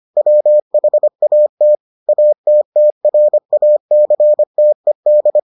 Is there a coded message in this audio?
Yes: Morse code from 0.2 to 5.4 s, heard as a keyed tone, about 25 wpm, 600 Hz. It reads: WHAT ATTRACTED